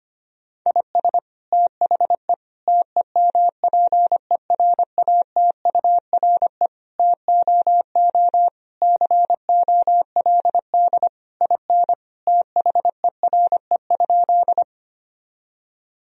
Morse code, heard as a keyed tone, about 25 wpm, 700 Hz. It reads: IS THE TEMPERATURE TOO COLD IN THERE?
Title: Sentences from Top 1000 Words - Review Part 4 - 25wpm